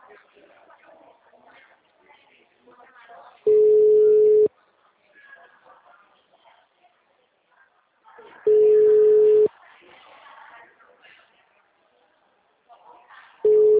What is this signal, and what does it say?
Ringback tone (Europe, 425 Hz), with speech.
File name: out- -2006- - - .83679